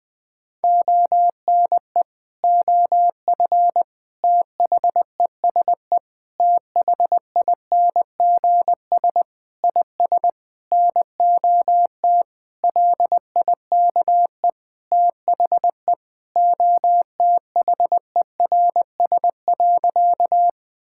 Morse code, heard as a keyed tone, about 20 wpm, 700 Hz. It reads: ONE OF THESE THINGS IS NOT LIKE THE OTHERS.
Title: Sentences with 8-Word Phrases - 20wpm